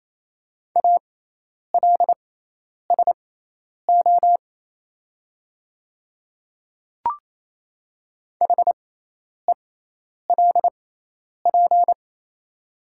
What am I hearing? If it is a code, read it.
Morse code, heard as a keyed tone, about 27 wpm, spaced as for 10 wpm, 700 Hz. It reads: ALSO HELP
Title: 6x Character Spacing - Top 200 Words - 28wpm